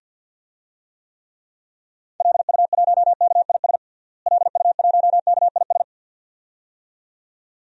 Morse code, heard as a keyed tone, about 52 wpm, 700 Hz. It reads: LU1KIS LU1KIS
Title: DXCC-100 Call Signs - 6 Characters - Repeated 2x - 50wpm